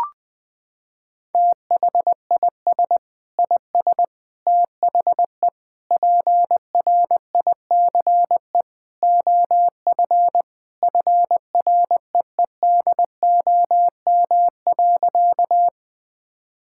Morse code, heard as a keyed tone, about 20 wpm, 700 Hz. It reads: THIS IS THE PRICE OF FREEDOM.
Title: Sentences from Top 1000 Words - Review Part 5 - 20wpm